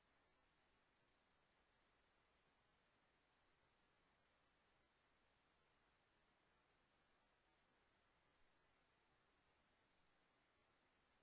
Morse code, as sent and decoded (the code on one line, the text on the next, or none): none